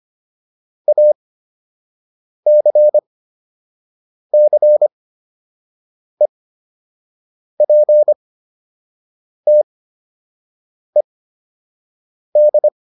Morse code, heard as a keyed tone, about 25 wpm, spaced as for 6 wpm, 600 Hz. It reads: ACCEPTED
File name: ED8 06